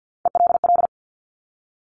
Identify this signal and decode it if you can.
Morse code, heard as a keyed tone, about 50 wpm, 700 Hz. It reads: ELL
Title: Top Three Letter Combinations in Words 50wpm